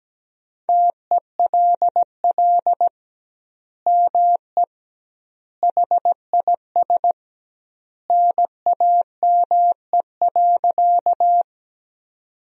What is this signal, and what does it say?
Morse code, heard as a keyed tone, about 17 wpm, 700 Hz. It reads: TELL ME HIS NAME.
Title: 2x Word-Spacing - Sentences from Top 200 Words - 17wpm